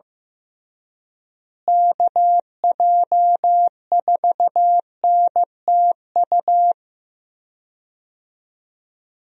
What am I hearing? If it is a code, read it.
Morse code, heard as a keyed tone, about 15 wpm, 700 Hz. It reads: KJ4NTU
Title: USA Call Signs - 6 Characters - Repeated 3x - 15wpm